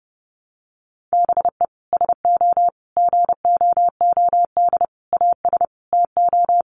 Morse code, heard as a keyed tone, about 30 wpm, 700 Hz. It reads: BE SO GOOD AS TO
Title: Sentences from Top 900 Words 30wpm